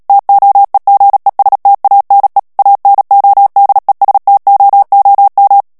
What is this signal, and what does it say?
Morse code, heard as a keyed tone, about 37 wpm, 800 Hz. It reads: TOEGESTANE ANODESTOOM
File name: morse mp3 / HOLLANDAIS texte clair 2 vitesse 35 mots minute